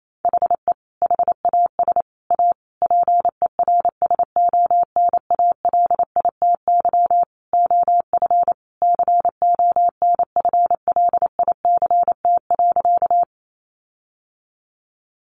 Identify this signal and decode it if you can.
Morse code, heard as a keyed tone, about 28 wpm, 700 Hz. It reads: HE HAS A PERSONALITY OF CONFLICT.
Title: Sentences from Top 1400 Words 28wpm